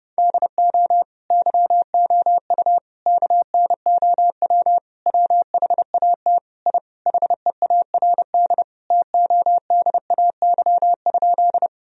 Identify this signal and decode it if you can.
Morse code, heard as a keyed tone, about 32 wpm, 700 Hz. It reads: DO YOU KNOW WHAT I HEARD TODAY?
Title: Sentences from Top 1000 Words - Review Part 4 - 30wpm